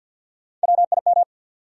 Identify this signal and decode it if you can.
Morse code, heard as a keyed tone, about 52 wpm, 700 Hz. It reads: WIM
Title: USA Call Sign Postfixes - 3 Letters 50wpm